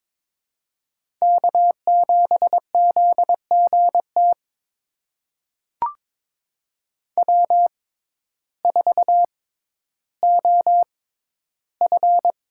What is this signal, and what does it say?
Morse code, heard as a keyed tone, about 22 wpm, spaced as for 12 wpm, 700 Hz. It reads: K7ZGT W4OF
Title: USA Call Signs - 5 Characters - 6x Character Spacing - 22wpm